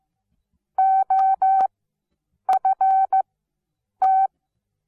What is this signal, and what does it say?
Morse code, heard as a keyed tone, about 15 wpm, spaced as for 8 wpm, 770 Hz. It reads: OFT
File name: Onlinelehrgang / morsen / DJ4UF Morsen17